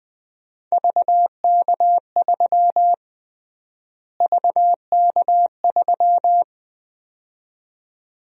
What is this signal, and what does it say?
Morse code, heard as a keyed tone, about 20 wpm, 700 Hz. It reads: VK3 VK3
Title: USA Call Sign Prefixes - Repeated 2x - 20wpm